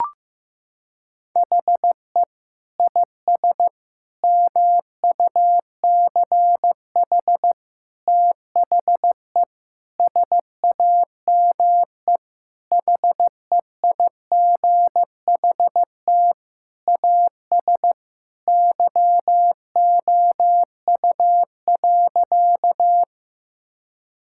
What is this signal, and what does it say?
Morse code, heard as a keyed tone, about 15 wpm, 700 Hz. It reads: HE IS MUCH THE SAME HEIGHT AS YOU.
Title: Sentences from Top 1600 Words 15wpm